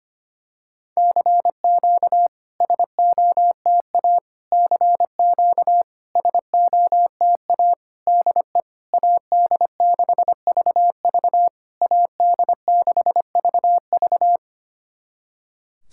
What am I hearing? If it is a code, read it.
Morse code, heard as a keyed tone, about 25 wpm, 690 Hz. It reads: CQ SOTA CQ SOTA DE AD6VV AD6VV
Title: SOTA QSO Practice 25wpm